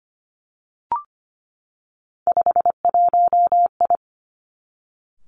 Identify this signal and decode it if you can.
Morse code, heard as a keyed tone, about 25 wpm, 700 Hz. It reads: 51I
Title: Three Challenging Letter-Number Combinations 25wpm